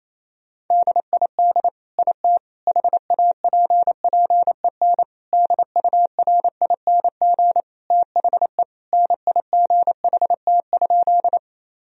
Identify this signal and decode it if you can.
Morse code, heard as a keyed tone, about 27 wpm, 700 Hz. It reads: DID IT HAPPEN DURING THE NIGHT?
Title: Sentences from Top 1000 Words - Review Part 2 - 28wpm